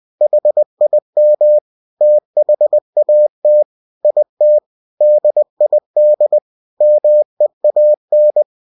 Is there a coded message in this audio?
Yes: Morse code from 0.2 to 8.4 s, heard as a keyed tone, about 20 wpm, 600 Hz. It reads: HIM THAT IT DID MEAN